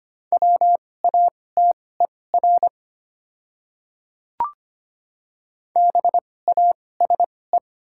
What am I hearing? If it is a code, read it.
Morse code, heard as a keyed tone, about 25 wpm, spaced as for 18 wpm, 700 Hz. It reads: WATER BASE
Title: 2x Character Spacing - Top 300 Words - 25wpm